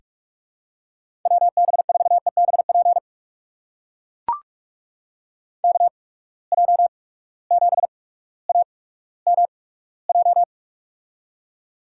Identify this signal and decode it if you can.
Morse code, heard as a keyed tone, about 42 wpm, spaced as for 19 wpm, 700 Hz. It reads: WB4EBP KJ7AMJ
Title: USA Call Signs - 6 Characters - 8x Character Spacing - 45wpm